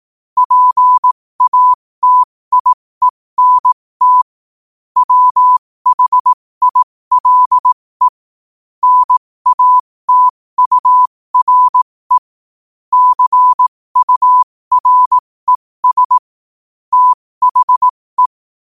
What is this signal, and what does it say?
Morse code, heard as a keyed tone, about 18 wpm, spaced as for 15 wpm, 1000 Hz. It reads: PATIENT WHILE NATURE CURES THE
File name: 15wpm